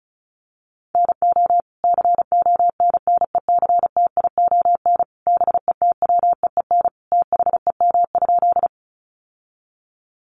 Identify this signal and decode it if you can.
Morse code, heard as a keyed tone, about 34 wpm, 700 Hz. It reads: NO CONNECTION BETWEEN THEM?
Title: Sentences from Top 1300 Words 35wpm